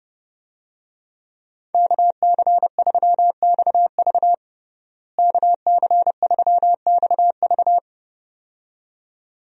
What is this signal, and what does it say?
Morse code, heard as a keyed tone, about 30 wpm, 700 Hz. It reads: KC3XV KC3XV
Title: USA Call Signs - 5 Characters - Repeated 2x - 30wpm